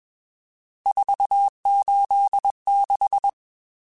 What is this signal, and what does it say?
Morse code, heard as a keyed tone, about 21 wpm, 800 Hz. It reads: 486
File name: RSTs / RST-mp3s / Various RST @20